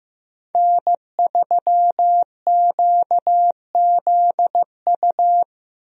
Morse code, heard as a keyed tone, about 16 wpm, 700 Hz. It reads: N3QZU
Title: USA Call Signs - 5 Characters - Repeated 3x - 15wpm